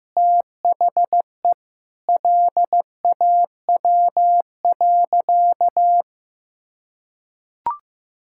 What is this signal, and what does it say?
Morse code, heard as a keyed tone, about 15 wpm, 700 Hz. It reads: THE LAW.
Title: Sentences from Top 1000 Words - Review Part 5 - 15wpm